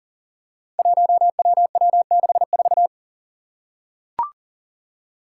Morse code, heard as a keyed tone, about 40 wpm, 700 Hz. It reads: 1WW64
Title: Five Challenging Letter-Number Combinations 40wpm